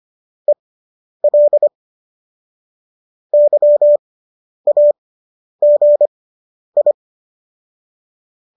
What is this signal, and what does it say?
Morse code, heard as a keyed tone, about 25 wpm, spaced as for 10 wpm, 590 Hz. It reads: EL YAGI